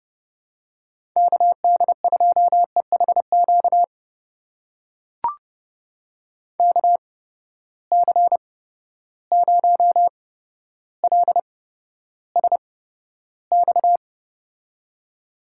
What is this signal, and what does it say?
Morse code, heard as a keyed tone, about 30 wpm, spaced as for 13 wpm, 700 Hz. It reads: KD2EHQ KC0LSX
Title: USA Call Signs - 6 Characters - 8x Character Spacing - 30wpm